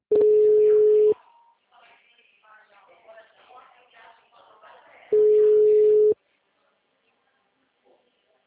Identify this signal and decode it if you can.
Ringback tone (Europe, 425 Hz), with speech.